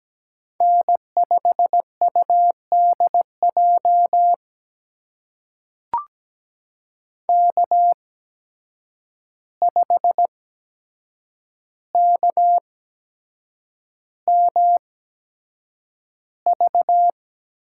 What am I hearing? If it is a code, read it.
Morse code, heard as a keyed tone, about 17 wpm, spaced as for 7 wpm, 700 Hz. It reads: N5UDJ K5KMV
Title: USA Call Signs - 5 Characters - 8x Character Spacing - 17wpm